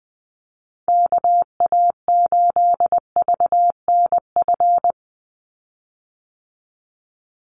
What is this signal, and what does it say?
Morse code, heard as a keyed tone, about 20 wpm, 700 Hz. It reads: KA8VNF
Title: USA Call Signs 20wpm